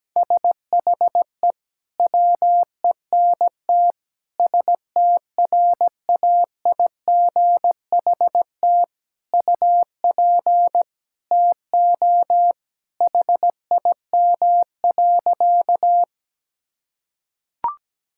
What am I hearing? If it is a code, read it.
Morse code, heard as a keyed tone, about 17 wpm, 700 Hz. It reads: SHE WENT STRAIGHT UP TO HIM.
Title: Sentences from Top 1000 Words - Review Part 6 - 17wpm